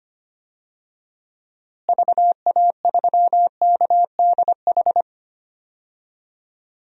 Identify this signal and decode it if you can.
Morse code, heard as a keyed tone, about 25 wpm, 700 Hz. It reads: VA3KDH
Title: DXCC-100 Call Signs - 6 Characters - Repeated 2x - 25wpm